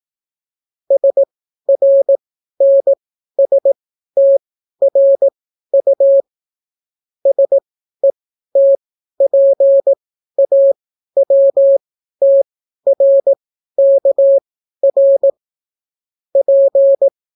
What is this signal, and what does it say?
Morse code, heard as a keyed tone, about 18 wpm, spaced as for 12 wpm, 550 Hz. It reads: SRNSTRU SETPAWTRKR P